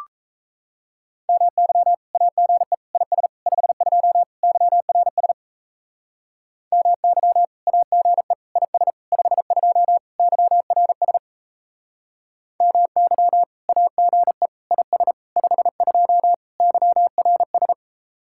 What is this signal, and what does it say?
Morse code, heard as a keyed tone, about 37 wpm, 700 Hz. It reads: MY AGE IS 52 YRS MY AGE IS 52 YRS MY AGE IS 52 YRS